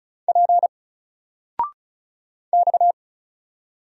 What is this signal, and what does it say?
Morse code, heard as a keyed tone, about 36 wpm, 700 Hz. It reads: P X
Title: Single Letter-Number 35wpm